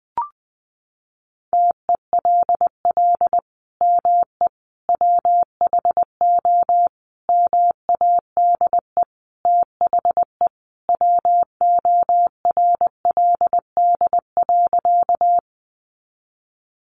Morse code, heard as a keyed tone, about 20 wpm, 700 Hz. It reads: TELL ME WHO MADE THE WORLD.
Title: Sentences from Top 200 Words 20wpm